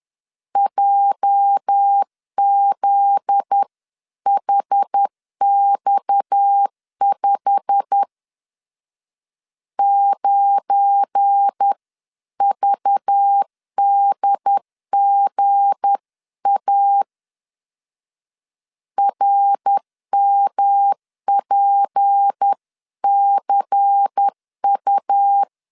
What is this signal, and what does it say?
Morse code, heard as a keyed tone, about 10 wpm, 800 Hz. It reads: JZHX5 9VDGA RMPCU
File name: CW / DET / 1 9 1500C S A 10wpm 4511